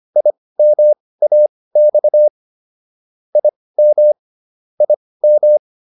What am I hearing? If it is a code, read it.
Morse code, heard as a keyed tone, about 25 wpm, spaced as for 18 wpm, 600 Hz. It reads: IMAX IM IM